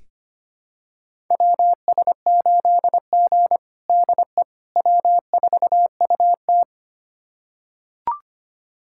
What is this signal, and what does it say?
Morse code, heard as a keyed tone, about 25 wpm, 700 Hz. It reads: WS8G DE W4UT
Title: USA Callsign DE Callsign - 4 Characters - 25wpm